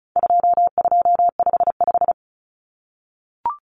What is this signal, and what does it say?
Morse code, heard as a keyed tone, about 34 wpm, 700 Hz. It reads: 2255
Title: Numbers - 4-Digit 35wpm